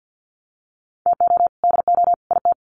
Morse code, heard as a keyed tone, about 50 wpm, 700 Hz. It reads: TO DO IT
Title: Sets of 3 Words 50wpm